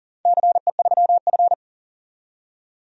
Morse code, heard as a keyed tone, about 38 wpm, 700 Hz. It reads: CE3F
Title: DXCC-100 Call Signs - 4 Characters - 40wpm